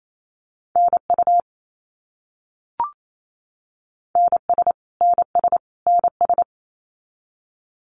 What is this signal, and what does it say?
Morse code, heard as a keyed tone, about 28 wpm, 700 Hz. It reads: NU NS NS NS
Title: Canadian Provinces and Territories - Repeated 3x - 28wpm